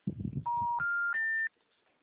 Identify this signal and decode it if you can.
Special information tone, with noise.